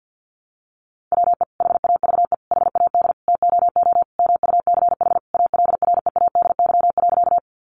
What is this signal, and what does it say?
Morse code, heard as a keyed tone, about 50 wpm, 700 Hz. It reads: WE HAVE HAD TOO MUCH ALREADY.
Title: Sentences from Top 700 Words 50wpm